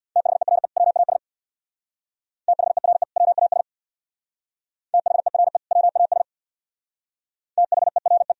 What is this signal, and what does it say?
Morse code, heard as a keyed tone, about 64 wpm, 700 Hz. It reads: THERE WAS THERE WAS THERE WAS THERE